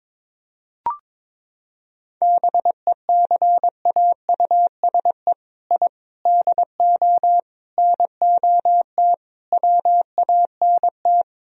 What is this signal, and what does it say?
Morse code, heard as a keyed tone, about 22 wpm, 700 Hz. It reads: BECAUSE I DO NOT WANT
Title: Sentences from Top 1000 Words - Review Part 3 - 22wpm